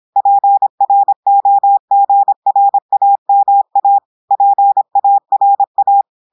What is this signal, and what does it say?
Morse code, heard as a keyed tone, about 26 wpm, 800 Hz. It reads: PROGRAMA PARA
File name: portugais texte en clair 2 vitesse 26 mots minutes